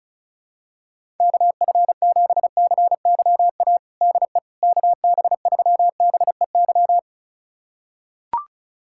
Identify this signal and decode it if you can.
Morse code, heard as a keyed tone, about 36 wpm, 700 Hz. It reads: KF7CYA DE KB3BEY
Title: USA Callsign DE Callsign - 6 Characters - 35wpm